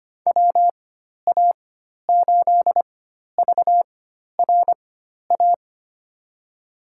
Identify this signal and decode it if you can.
Morse code, heard as a keyed tone, about 25 wpm, spaced as for 12 wpm, 700 Hz. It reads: WA8VRA
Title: USA Call Signs - 4x Character Spacing - 25wpm